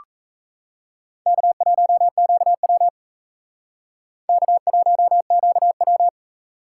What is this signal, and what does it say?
Morse code, heard as a keyed tone, about 35 wpm, 700 Hz. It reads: K1QW K1QW
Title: Speed Racing - USA 1x2 Call Signs 28wpm